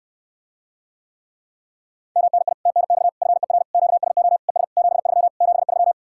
Morse code, heard as a keyed tone, about 67 wpm, 700 Hz. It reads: MNI TNX FER QSO ES 73 73